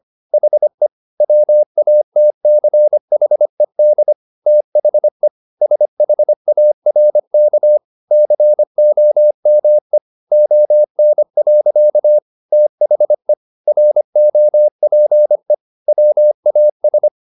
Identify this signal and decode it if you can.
Morse code, heard as a keyed tone, about 25 wpm, 600 Hz. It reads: HE WATCHED THE SHARK COME ON. THE ROPE WAS